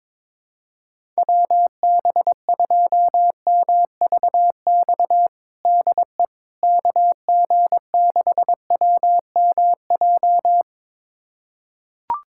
Morse code, heard as a keyed tone, about 22 wpm, 700 Hz. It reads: WB2MVX DE KG6WMJ
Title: USA Callsign DE Callsign - 6 Characters - 22wpm